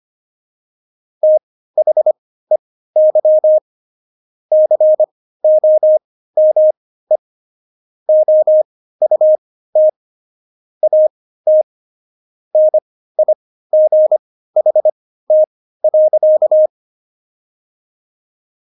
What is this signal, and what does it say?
Morse code, heard as a keyed tone, about 25 wpm, spaced as for 15 wpm, 620 Hz. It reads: THEY COME OUT AT NIGHT.
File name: Jokes-25-char-15-eff